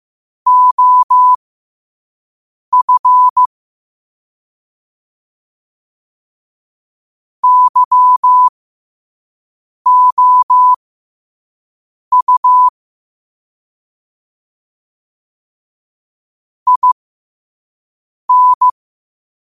Morse code, heard as a keyed tone, about 15 wpm, spaced as for 5 wpm, 1000 Hz. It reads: OF YOU IN
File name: qotdfiles / 5wpm